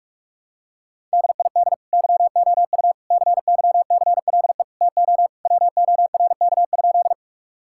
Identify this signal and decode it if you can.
Morse code, heard as a keyed tone, about 46 wpm, 700 Hz. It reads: DID YOU CYCLE TO WORK?
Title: Sentences from Top 1000 Words - Review Part 6 - 45wpm